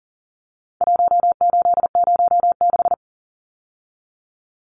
Morse code, heard as a keyed tone, about 40 wpm, 700 Hz. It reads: 1806
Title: Numbers - 4-Digit 40wpm